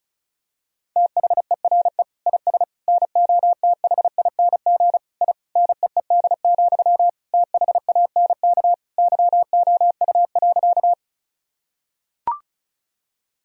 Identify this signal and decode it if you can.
Morse code, heard as a keyed tone, about 34 wpm, 700 Hz. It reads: THERE IS NOTHING I NEED, THANK YOU.
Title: Sentences from Top 1000 Words - Review Part 1 - 35wpm